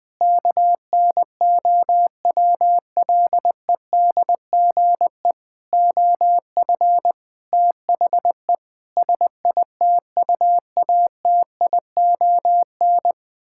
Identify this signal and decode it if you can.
Morse code, heard as a keyed tone, about 20 wpm, 700 Hz. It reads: KNOWLEDGE OF THE SITUATION
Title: Sentences with 9-Word Phrases - Spoken Beginning - 20wpm